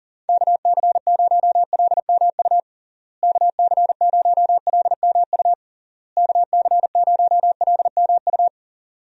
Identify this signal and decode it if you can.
Morse code, heard as a keyed tone, about 40 wpm, 700 Hz. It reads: KC0LMU KC0LMU KC0LMU
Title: USA Call Signs - 6 Characters - Repeated 3x - 40wpm